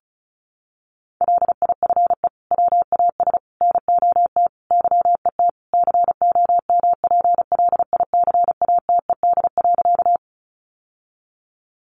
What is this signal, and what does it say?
Morse code, heard as a keyed tone, about 34 wpm, 700 Hz. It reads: LIFE WAS NOT YET COMPLICATED.
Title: Sentences from Top 1200 Words 35wpm